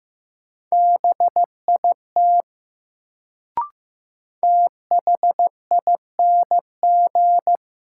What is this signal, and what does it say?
Morse code, heard as a keyed tone, about 15 wpm, 700 Hz. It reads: BIT THING
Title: Sets of 3 Words - New Word Drill - 15wpm